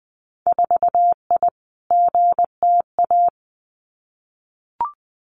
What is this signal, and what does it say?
Morse code, heard as a keyed tone, about 20 wpm, 700 Hz. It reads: 4I GTA
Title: Winter Field Day Exchanges Encore 20wpm